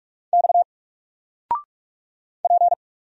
Morse code, heard as a keyed tone, about 46 wpm, 700 Hz. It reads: X P
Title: Single Letter-Number 45wpm